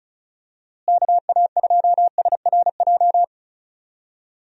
Morse code, heard as a keyed tone, about 34 wpm, 700 Hz. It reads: KA2SRJ
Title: USA Call Signs - 6 Characters - Repeated 3x - 35wpm